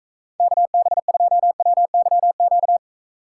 Morse code, heard as a keyed tone, about 35 wpm, 700 Hz. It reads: KB2WYQ
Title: Speed Racing - USA 2x3 Call Signs 28wpm